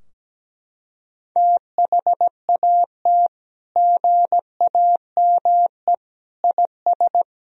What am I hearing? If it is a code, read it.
Morse code, heard as a keyed tone, about 17 wpm, 700 Hz. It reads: THAT GAME IS
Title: Sentences from Top 1000 Words - Review Part 3 - 17wpm